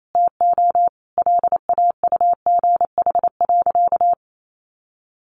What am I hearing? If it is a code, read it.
Morse code, heard as a keyed tone, about 28 wpm, 700 Hz. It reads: TO LAUGH.
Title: Sentences from Top 1100 Words 28wpm